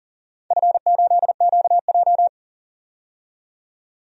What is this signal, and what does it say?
Morse code, heard as a keyed tone, about 40 wpm, 700 Hz. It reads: F8QJ
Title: DXCC-100 Call Signs - 4 Characters - 40wpm